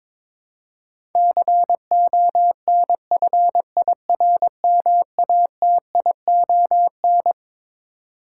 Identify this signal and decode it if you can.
Morse code, heard as a keyed tone, about 22 wpm, 700 Hz. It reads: CONFIRMATION
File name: Twelve Letter Words - 22wpm